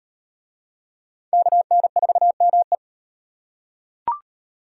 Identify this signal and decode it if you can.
Morse code, heard as a keyed tone, about 36 wpm, 700 Hz. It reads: KN4ME